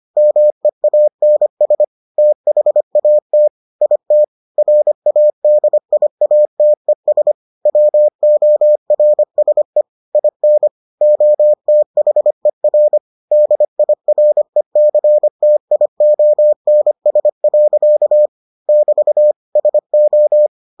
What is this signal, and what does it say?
Morse code, heard as a keyed tone, about 25 wpm, 600 Hz. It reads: MEANS THAT IT RADIATES WORSE IN OTHER DIRECTIONS. <BT> SO